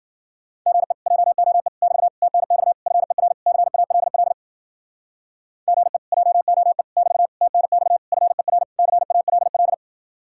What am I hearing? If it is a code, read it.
Morse code, heard as a keyed tone, about 50 wpm, 700 Hz. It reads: GE JOE <BT> TNX FER CALL GE JOE <BT> TNX FER CALL